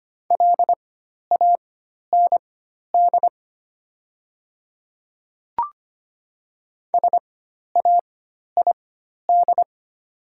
Morse code, heard as a keyed tone, about 25 wpm, spaced as for 12 wpm, 700 Hz. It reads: LAND SAID
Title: 4x Character Spacing - Top 300 Words - 25wpm